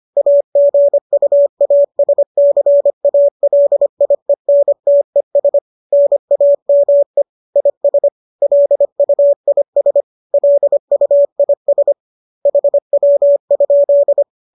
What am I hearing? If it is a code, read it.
Morse code, heard as a keyed tone, about 25 wpm, 570 Hz. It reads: AGUASCALIENTES NAME IS LUIS LUIS HW?
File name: qso204 25